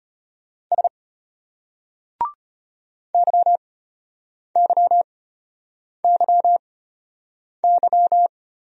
Morse code, heard as a keyed tone, about 33 wpm, spaced as for 8 wpm, 700 Hz. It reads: S YYYY